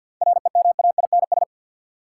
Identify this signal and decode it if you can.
Morse code, heard as a keyed tone, about 51 wpm, 700 Hz. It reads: REMAINS
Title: Sets of 5 Words Encore - New Word Drill - 50wpm